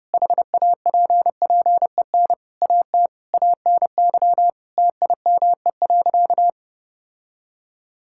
Morse code, heard as a keyed tone, about 30 wpm, 700 Hz. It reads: HAPPEN AT ANY TIME.
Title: Sentences from Top 1000 Words - Review Part 6 - 30wpm